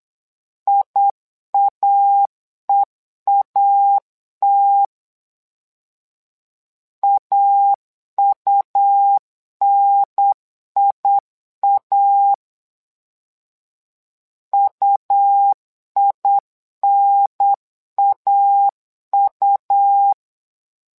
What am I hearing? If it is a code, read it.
Morse code, heard as a keyed tone, about 8 wpm, 800 Hz. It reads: IAEAT AUNIA UINAU
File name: LEZIONI / 2 UHDN / 2 UHDN 8wpm2232